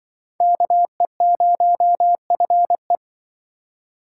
Morse code, heard as a keyed tone, about 25 wpm, 700 Hz. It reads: KE0FE